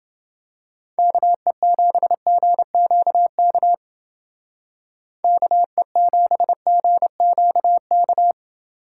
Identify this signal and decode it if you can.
Morse code, heard as a keyed tone, about 30 wpm, 700 Hz. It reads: KE7GQK KE7GQK